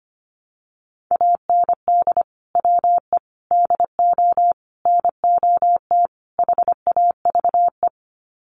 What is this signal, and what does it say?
Morse code, heard as a keyed tone, about 25 wpm, 700 Hz. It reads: AND WE DO NOT HAVE
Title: Sets of 5 Words Encore 25wpm